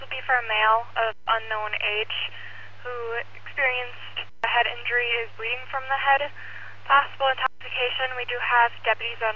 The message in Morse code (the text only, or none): none